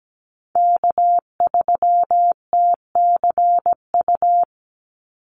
Morse code, heard as a keyed tone, about 17 wpm, 700 Hz. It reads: K3TCU
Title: NA SOTA Chaser Callsigns 17wpm